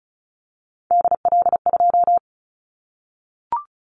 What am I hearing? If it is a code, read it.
Morse code, heard as a keyed tone, about 36 wpm, 700 Hz. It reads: DL2
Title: Callsign Prefixes for Top 50 DXCC - 35wpm